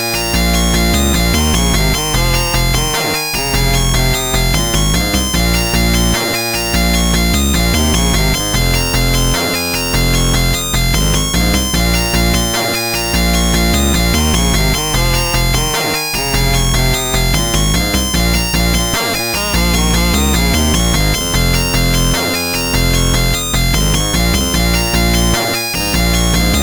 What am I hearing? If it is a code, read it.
There is no code.